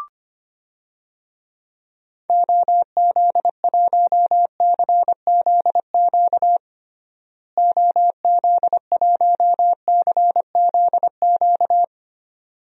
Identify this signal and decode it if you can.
Morse code, heard as a keyed tone, about 25 wpm, 700 Hz. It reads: OZ1CZQ OZ1CZQ
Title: DXCC-100 Call Signs - 6 Characters - Repeated 2x - 25wpm